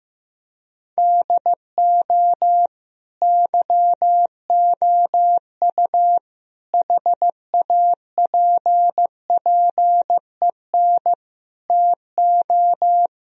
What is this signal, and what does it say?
Morse code, heard as a keyed tone, about 15 wpm, 700 Hz. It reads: DO YOU HAPPEN TO
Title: Sentences from Top 1000 Words - Review Part 2 - 15wpm